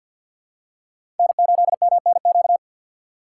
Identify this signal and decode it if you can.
Morse code, heard as a keyed tone, about 50 wpm, 700 Hz. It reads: N8MNQ
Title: USA Call Signs - 5 Characters - 2x Character Spacing - 50wpm